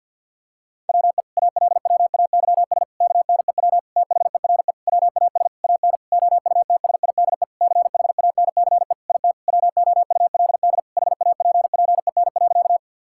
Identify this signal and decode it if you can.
Morse code, heard as a keyed tone, about 51 wpm, 700 Hz. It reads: WE ALWAYS KNEW THERE WAS AN OUTSIDE CHANCE IT WOULD HAPPEN.
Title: Sentences from Top 1000 Words - Review Part 6 - 50wpm